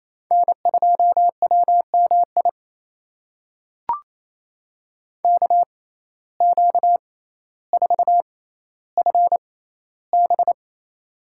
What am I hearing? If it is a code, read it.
Morse code, heard as a keyed tone, about 28 wpm, spaced as for 14 wpm, 700 Hz. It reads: N2WMI KQ4FB
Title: USA Call Signs - 5 Characters - 6x Character Spacing - 28wpm